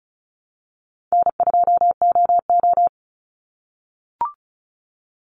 Morse code, heard as a keyed tone, about 34 wpm, 700 Hz. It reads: N2OO
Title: Call Signs - 35wpm